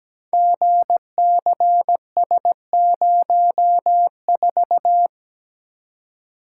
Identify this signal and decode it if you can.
Morse code, heard as a keyed tone, about 17 wpm, 700 Hz. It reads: GCS04
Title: Code Groups - 5 Letter:Numbers - 17wpm